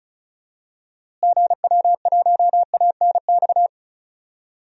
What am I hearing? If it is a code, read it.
Morse code, heard as a keyed tone, about 36 wpm, 700 Hz. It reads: GW1ANX
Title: DXCC-100 Call Signs - 6 Characters - Repeated 3x - 35wpm